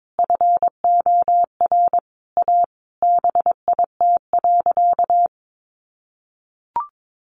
Morse code, heard as a keyed tone, about 22 wpm, 700 Hz. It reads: FOR A BIT.
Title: Sentences from Top 900 Words 22wpm